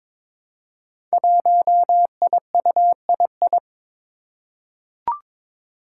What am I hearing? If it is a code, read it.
Morse code, heard as a keyed tone, about 22 wpm, 700 Hz. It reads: 1IUII
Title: Five Challenging Letter-Number Combinations 22wpm